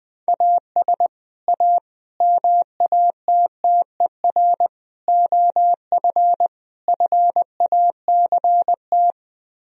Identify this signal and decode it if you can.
Morse code, heard as a keyed tone, about 20 wpm, 700 Hz. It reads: AS A MATTER OF FACT
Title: Speed-Racing - Sets of 5 Words 20wpm